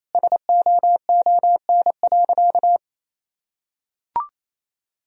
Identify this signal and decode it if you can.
Morse code, heard as a keyed tone, about 28 wpm, 700 Hz. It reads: SOON.
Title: Sentences from Top 1000 Words - Review Part 1 - 28wpm